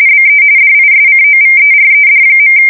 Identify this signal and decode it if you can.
RTTY, about 45.45 baud. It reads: BE DISCOVERED.
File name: fsk170 / battleshipnj